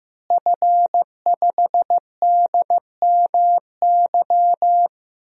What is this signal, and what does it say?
Morse code, heard as a keyed tone, about 15 wpm, 700 Hz. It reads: F5DMY
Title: DXCC-100 Call Signs - 5 Characters - Repeated 2x - 15wpm